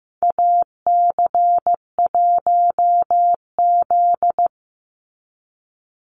Morse code, heard as a keyed tone, about 15 wpm, 700 Hz. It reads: AC1Z
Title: NA SOTA Activator Callsigns 15wpm